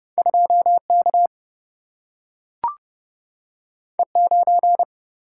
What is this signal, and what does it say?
Morse code, heard as a keyed tone, about 30 wpm, 700 Hz. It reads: 2 K E 9
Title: Single-Letter Single-Number 30wpm